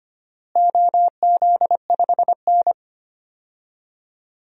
Morse code, heard as a keyed tone, about 25 wpm, 700 Hz. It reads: OZ5N
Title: DXCC-100 Call Signs - 4 Characters - Repeated 3x - 25wpm